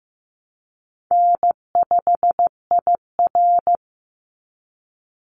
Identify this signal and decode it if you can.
Morse code, heard as a keyed tone, about 15 wpm, 700 Hz. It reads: N5IR
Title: Call Signs Repeated 3x - 15wpm